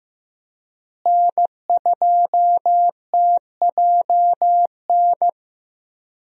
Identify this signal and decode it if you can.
Morse code, heard as a keyed tone, about 15 wpm, 700 Hz. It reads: N2TJN
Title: USA Call Signs - 5 Characters - Repeated 3x - 15wpm